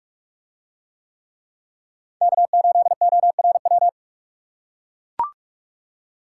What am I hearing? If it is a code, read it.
Morse code, heard as a keyed tone, about 41 wpm, 700 Hz. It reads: K8ORW